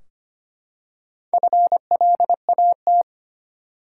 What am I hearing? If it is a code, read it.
Morse code, heard as a keyed tone, about 25 wpm, 700 Hz. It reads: FLAT
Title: Four Letter Words - 25wpm